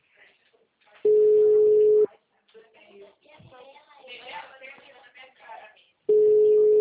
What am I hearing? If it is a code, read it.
Ringback tone (Europe, 425 Hz), with speech.